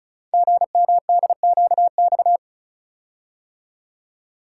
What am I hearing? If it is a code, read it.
Morse code, heard as a keyed tone, about 36 wpm, 700 Hz. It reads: GMDQX
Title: Code Groups - 5 Letters - 35wpm